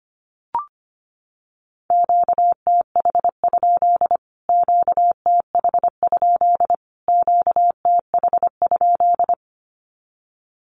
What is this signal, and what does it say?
Morse code, heard as a keyed tone, about 25 wpm, 700 Hz. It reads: QTH? QTH? QTH?
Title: Q-Codes Repeated 3x 25wpm